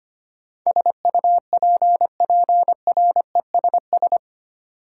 Morse code, heard as a keyed tone, about 25 wpm, 700 Hz. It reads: SUPPRESS
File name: Eight Letter Words - 25wpm